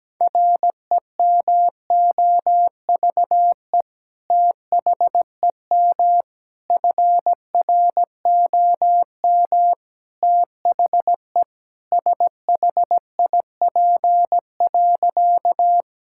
Morse code, heard as a keyed tone, about 17 wpm, 700 Hz. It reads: REMOVE THEM FROM THE SHIP.
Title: Sentences from Top 1000 Words - Review Part 5 - 17wpm